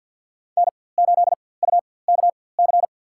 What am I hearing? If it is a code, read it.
Morse code, heard as a keyed tone, about 52 wpm, spaced as for 23 wpm, 700 Hz. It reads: N8UKC
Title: USA Call Signs - 5 Characters - 4x Character Spacing - 50wpm